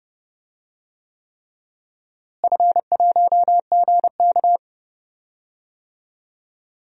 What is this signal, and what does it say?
Morse code, heard as a keyed tone, about 30 wpm, 700 Hz. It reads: F1GK
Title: DXCC-100 Call Signs - 4 Characters - Repeated 3x - 30wpm